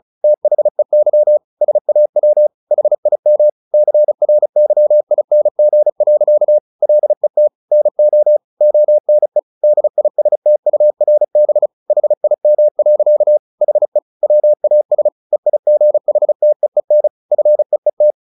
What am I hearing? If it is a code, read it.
Morse code, heard as a keyed tone, about 34 wpm, 600 Hz. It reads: THEY SAW HIM CRYING. LET NO ONE DISTURB HIM. HE WAS EIGHTEEN FEET
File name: the old man and the sea 2 35wpm